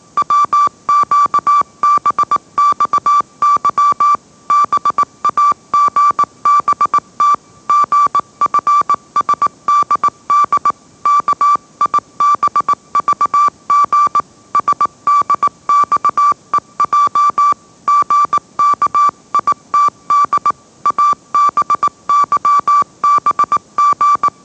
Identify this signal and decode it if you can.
Morse code, heard as a keyed tone, about 18 wpm, 1160 Hz. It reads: WQBXY BAGBT GFSDD KIBVG SDXEJ GKITD ABYBG
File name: mrx / mrx081 20wpm